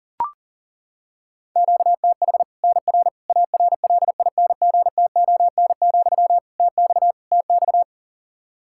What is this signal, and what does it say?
Morse code, heard as a keyed tone, about 40 wpm, 700 Hz. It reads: QTH NR ARLINGTON, TX TX
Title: K1USN SST QSO Practice 40wpm